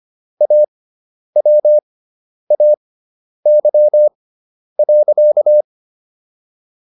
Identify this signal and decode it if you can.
Morse code, heard as a keyed tone, about 25 wpm, spaced as for 10 wpm, 600 Hz. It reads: AWAY.